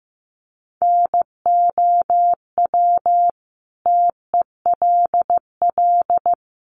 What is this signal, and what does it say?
Morse code, heard as a keyed tone, about 15 wpm, 700 Hz. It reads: NOW TELL
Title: Sentences from Top 200 Words 15wpm